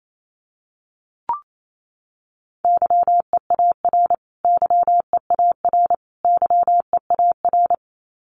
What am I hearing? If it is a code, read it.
Morse code, heard as a keyed tone, about 28 wpm, 700 Hz. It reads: YEAR YEAR YEAR
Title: Top 200 Words Repeated 28wpm